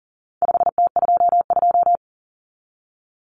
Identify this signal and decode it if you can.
Morse code, heard as a keyed tone, about 40 wpm, 700 Hz. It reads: 5T22
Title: Common Cut Numbers - 4-Digit 40wpm